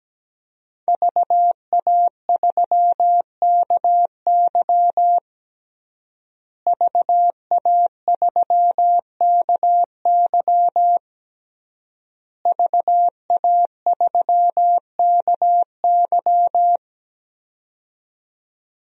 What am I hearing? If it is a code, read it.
Morse code, heard as a keyed tone, about 17 wpm, 700 Hz. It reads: VA3KY VA3KY VA3KY
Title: DXCC-100 Call Signs - 5 Characters - Repeated 3x - 17wpm